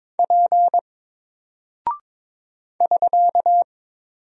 Morse code, heard as a keyed tone, about 22 wpm, 700 Hz. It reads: P <SK>
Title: Single Character (Letter:Number:Punctuation:Prosign) Alt 22wpm